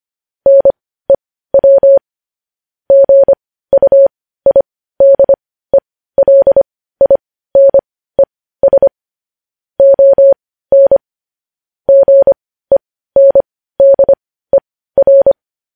Morse code, heard as a keyed tone, about 25 wpm, spaced as for 15 wpm, 550 Hz. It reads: NEW GUIDELINES ON GENDER
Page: News Headlines in Morse Code at 15 WPM